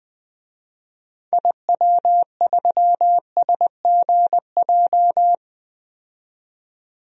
Morse code, heard as a keyed tone, about 20 wpm, 700 Hz. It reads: IW3SGJ
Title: DXCC-100 Call Signs - 6 Characters - 20wpm